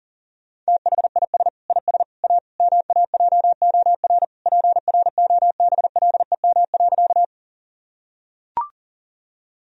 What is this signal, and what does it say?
Morse code, heard as a keyed tone, about 40 wpm, 700 Hz. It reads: THIS IS A MAJOR PROBLEM.
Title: Sentences from Top 1000 Words - Review Part 4 - 40wpm